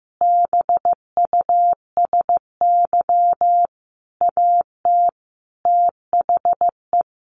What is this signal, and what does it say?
Morse code, heard as a keyed tone, about 15 wpm, 700 Hz. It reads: BUSY AT THE
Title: Sentences from Top 1500 Words 15wpm